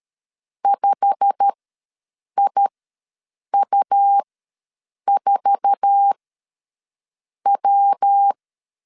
Morse code, heard as a keyed tone, about 13 wpm, spaced as for 6 wpm, 800 Hz. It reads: 5IU4W
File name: LEZIONI / 10 SIGNS / 10B INT ws 12wpm 3527